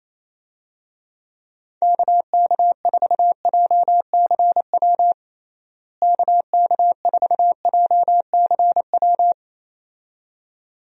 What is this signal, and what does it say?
Morse code, heard as a keyed tone, about 27 wpm, 700 Hz. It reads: KK4JCW KK4JCW
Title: USA Call Signs - 6 Characters - Repeated 2x - 28wpm